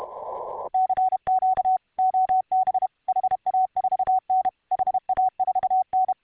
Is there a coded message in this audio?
Yes: Morse code from 0.7 to 6.1 s, heard as a keyed tone, about 32 wpm, 750 Hz. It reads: CQ OB HA4N HA4N